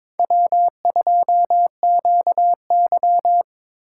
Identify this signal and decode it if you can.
Morse code, heard as a keyed tone, about 26 wpm, 700 Hz. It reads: W2QY
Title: Speed Racing - USA 1x2 Call Signs 22wpm